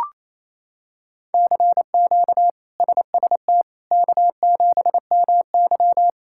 Morse code, heard as a keyed tone, about 27 wpm, 700 Hz. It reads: CQ SST K7MY
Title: K1USN SST QSO Practice - 28wpm